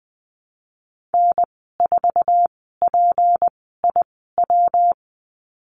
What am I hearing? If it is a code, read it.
Morse code, heard as a keyed tone, about 20 wpm, spaced as for 14 wpm, 700 Hz. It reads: N4PIW
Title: 193 Lesson 31g - Letter v - Call Signs 20wpm